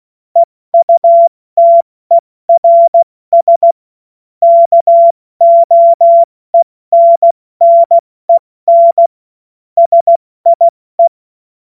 Morse code, heard as a keyed tone, about 16 wpm, 680 Hz. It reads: EUTERS KOENNEN SIE
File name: Morsekurs / Witze / Witz 24 Juni Mai 2025 70 F80